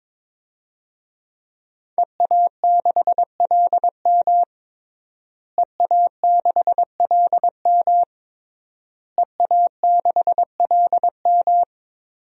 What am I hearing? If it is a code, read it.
Morse code, heard as a keyed tone, about 22 wpm, 700 Hz. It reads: EA6LM EA6LM EA6LM
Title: DXCC-100 Call Signs - 5 Characters - Repeated 3x - 22wpm